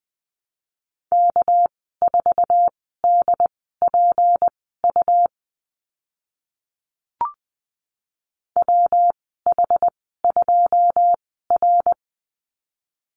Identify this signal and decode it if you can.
Morse code, heard as a keyed tone, about 20 wpm, spaced as for 14 wpm, 700 Hz. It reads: K4DPU WH2R
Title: 200 Lesson 32g - Letter k - Call Signs 20wpm